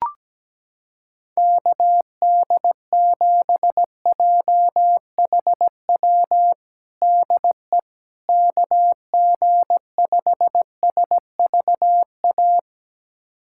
Morse code, heard as a keyed tone, about 17 wpm, 700 Hz. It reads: KD7JHW DE KG5SVA
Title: USA Callsign DE Callsign - 6 Characters - 17wpm